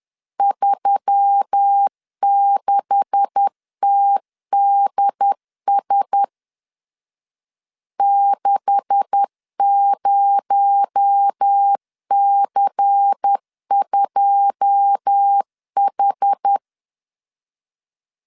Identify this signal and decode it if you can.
Morse code, heard as a keyed tone, about 10 wpm, 800 Hz. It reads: 36TDS 60C2H
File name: LEZIONI / 9 286 10wpm 2735